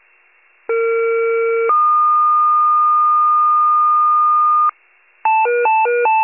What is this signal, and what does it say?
Two-tone page, heard as a one-second tone, then a three-second tone, about 460 then 1220 Hz. It is heard from 0.7 to 4.7 s.